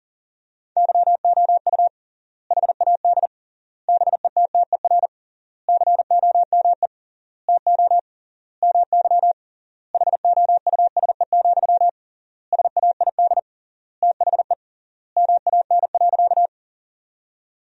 Morse code, heard as a keyed tone, about 40 wpm, 700 Hz. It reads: YOU HAD BETTER COME TO MY HOUSE, SAID THE MAN.
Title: 3x Word-Spacing - Sentences from Top 400 Words - 40wpm